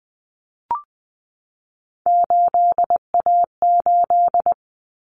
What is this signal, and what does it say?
Morse code, heard as a keyed tone, about 20 wpm, 700 Hz. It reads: 8A8
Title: Common Cut Numbers - 3-Digit 20wpm